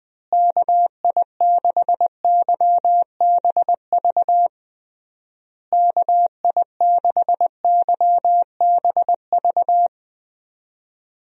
Morse code, heard as a keyed tone, about 20 wpm, 700 Hz. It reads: KI6YBV KI6YBV
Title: USA Call Signs - 6 Characters - Repeated 3x - 20wpm